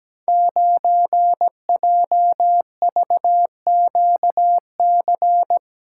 Morse code, heard as a keyed tone, about 17 wpm, 700 Hz. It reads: 9JVQC
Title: Code Groups - 5 Letter:Numbers - 17wpm